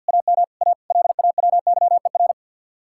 Morse code, heard as a keyed tone, about 64 wpm, 700 Hz. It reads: AM A LAWYER